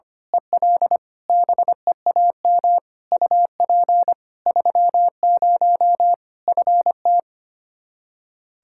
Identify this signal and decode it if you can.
Morse code, heard as a keyed tone, about 25 wpm, 700 Hz. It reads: EL BEAM UP 30 FT
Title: Practice QSO - Exchange 2 - Info - 25wpm